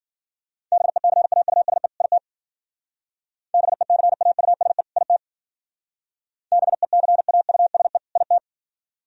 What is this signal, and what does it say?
Morse code, heard as a keyed tone, about 50 wpm, 700 Hz. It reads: BECAUSE IT BECAUSE IT BECAUSE IT
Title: Speed-Racing - Sets of 2 Words 40wpm